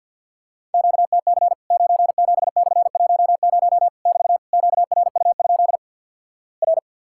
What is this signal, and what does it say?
Morse code, heard as a keyed tone, about 50 wpm, 700 Hz. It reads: QTC 97/10 <BT> QRU? R
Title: Worked All Europe DX Contest - QTC Exchange - 50wpm